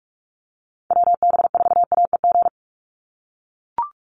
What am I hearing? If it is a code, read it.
Morse code, heard as a keyed tone, about 44 wpm, 700 Hz. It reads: WB4AEG
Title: Call Signs - 45wpm